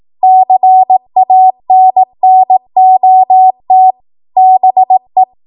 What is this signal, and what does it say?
Morse code, heard as a keyed tone, about 18 wpm, 750 Hz. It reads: CANNOT BE
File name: Morse / 18WPM